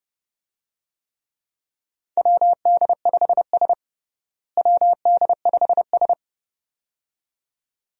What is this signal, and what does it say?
Morse code, heard as a keyed tone, about 29 wpm, 700 Hz. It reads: WD5S WD5S
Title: USA Call Signs - 4 Characters - Repeated 2x - 30wpm